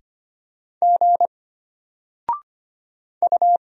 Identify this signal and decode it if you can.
Morse code, heard as a keyed tone, about 25 wpm, 700 Hz. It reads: G U
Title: Single Letter-Number Alt 25wpm